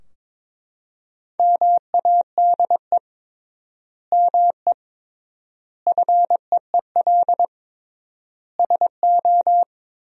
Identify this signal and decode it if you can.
Morse code, heard as a keyed tone, about 22 wpm, 700 Hz. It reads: MADE ME FEEL SO
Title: 3x Word-Spacing - Sentences from Top 400 Words - 22wpm